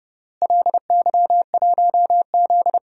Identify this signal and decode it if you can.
Morse code, heard as a keyed tone, about 30 wpm, 700 Hz. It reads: LY1Z
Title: DXCC-100 Call Signs - 4 Characters - Repeated 3x - 30wpm